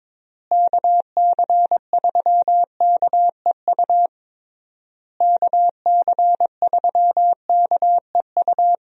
Morse code, heard as a keyed tone, about 21 wpm, 700 Hz. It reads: KC3KEU KC3KEU
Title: USA Call Signs - 6 Characters - Repeated 3x - 22wpm